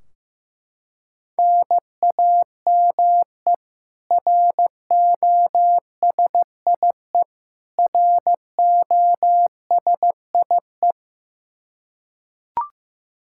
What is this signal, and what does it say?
Morse code, heard as a keyed tone, about 15 wpm, 700 Hz. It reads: NAME ROSIE ROSIE
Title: K1USN SST QSO Practice 15wpm